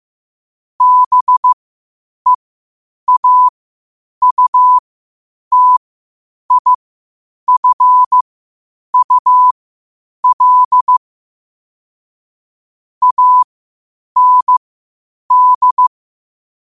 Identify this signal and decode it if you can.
Morse code, heard as a keyed tone, about 15 wpm, spaced as for 8 wpm, 1000 Hz. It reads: BEAUTIFUL AND